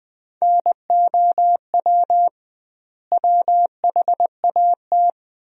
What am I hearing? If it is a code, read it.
Morse code, heard as a keyed tone, about 21 wpm, 700 Hz. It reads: NOW WHAT
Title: 2x Word-Spacing - Sentences from Top 100 Words - 20wpm